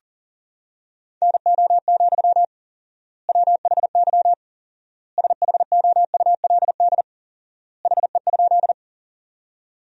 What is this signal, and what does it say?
Morse code, heard as a keyed tone, about 40 wpm, 700 Hz. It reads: NO, WHY SHOULD HE?
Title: 4x Word-Spacing - Sentences from Top 300 Words - 40wpm